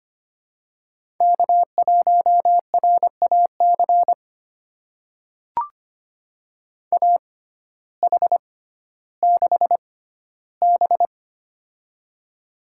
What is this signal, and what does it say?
Morse code, heard as a keyed tone, about 25 wpm, spaced as for 14 wpm, 700 Hz. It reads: K1RAC AH6B
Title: USA Call Signs - 6x Character Spacing - 25wpm